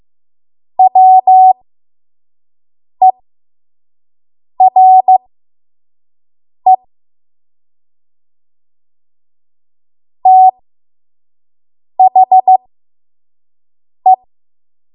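Morse code, heard as a keyed tone, about 15 wpm, spaced as for 5 wpm, 750 Hz. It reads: WERE THE
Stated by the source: 5 WPM Morse code practice.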